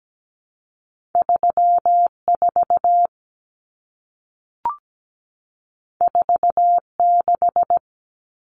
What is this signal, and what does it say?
Morse code, heard as a keyed tone, about 17 wpm, 700 Hz. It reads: 34 46
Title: Common Cut Numbers - 2-Digit 17wpm